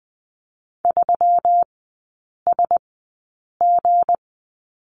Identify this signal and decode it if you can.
Morse code, heard as a keyed tone, about 20 wpm, spaced as for 8 wpm, 700 Hz. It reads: 3SG
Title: 184 Lesson 30e - Letter b - Three Characters 20wpm